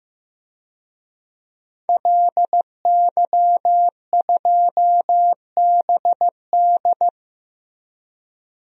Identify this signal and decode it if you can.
Morse code, heard as a keyed tone, about 15 wpm, 700 Hz. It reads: LY2BD
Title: DXCC-100 Call Signs - 5 Characters - Repeated 3x - 15wpm